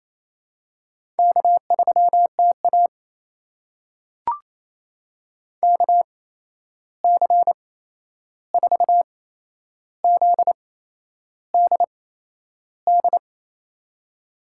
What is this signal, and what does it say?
Morse code, heard as a keyed tone, about 28 wpm, spaced as for 11 wpm, 700 Hz. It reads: K3TA KC4ZDD
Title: USA Call Signs - 8x Character Spacing - 28wpm